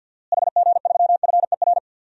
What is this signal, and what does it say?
Morse code, heard as a keyed tone, about 51 wpm, 700 Hz. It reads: HG3FER
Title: DXCC-100 Call Signs - 6 Characters - Repeated 2x - 50wpm